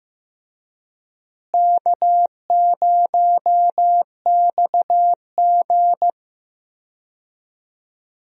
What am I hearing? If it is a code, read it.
Morse code, heard as a keyed tone, about 15 wpm, 700 Hz. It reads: K0XG
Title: USA Call Signs - 4 Characters - Repeated 3x - 15wpm